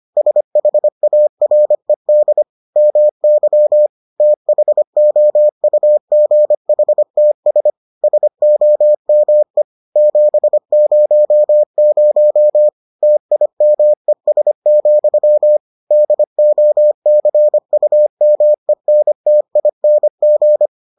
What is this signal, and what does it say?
Morse code, heard as a keyed tone, about 25 wpm, 600 Hz. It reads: SHARED MY THOUGHTS SOME 700 TIMES, DOCUMENTING